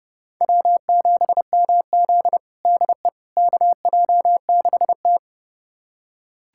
Morse code, heard as a keyed tone, about 30 wpm, 700 Hz. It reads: W7MZ DE KJ6T
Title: USA Callsign DE Callsign - 4 Characters - 30wpm